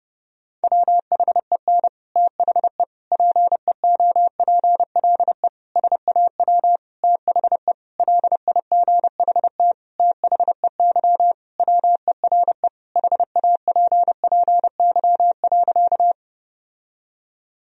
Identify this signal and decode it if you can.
Morse code, heard as a keyed tone, about 30 wpm, 700 Hz. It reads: WHEN THE PEOPLE SAW THE LIGHT THEY WERE HAPPY.
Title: Sentences from Top 1000 Words - Review Part 1 - 30wpm